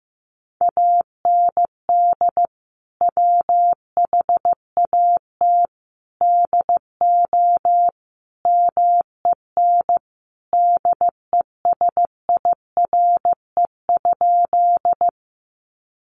Morse code, heard as a keyed tone, about 15 wpm, 700 Hz. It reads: AND WHAT DO MEN DESIRE?
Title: Sentences from Top 1100 Words 15wpm